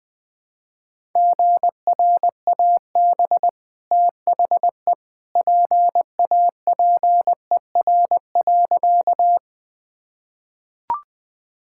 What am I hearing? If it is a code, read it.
Morse code, heard as a keyed tone, about 21 wpm, 700 Hz. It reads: GRAB THE PAPER.
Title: Easy Sentences with Hints - 20wpm